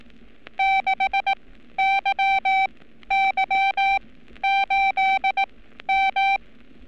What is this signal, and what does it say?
Morse code, heard as a keyed tone, about 18 wpm, spaced as for 12 wpm, 780 Hz. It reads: 6YY8M
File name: training / NAVPERS 11224-11235 / NAVPERS 11227 RA RB